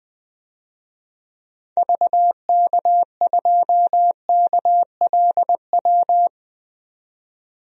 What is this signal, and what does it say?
Morse code, heard as a keyed tone, about 20 wpm, 700 Hz. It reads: VK2KLW
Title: DXCC-100 Call Signs - 6 Characters - 20wpm